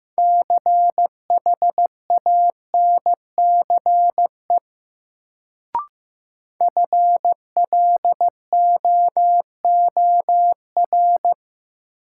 Morse code, heard as a keyed tone, about 15 wpm, 700 Hz. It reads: CHANCE FLOOR
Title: Sets of 3 Words - New Word Drill - 15wpm